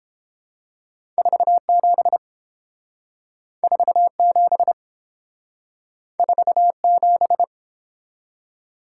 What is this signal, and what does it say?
Morse code, heard as a keyed tone, about 30 wpm, 700 Hz. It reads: 47 47 47